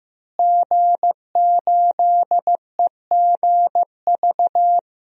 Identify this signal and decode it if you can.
Morse code, heard as a keyed tone, about 15 wpm, 700 Hz. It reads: G8EGV
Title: DXCC-100 Call Signs - 5 Characters - Repeated 2x - 15wpm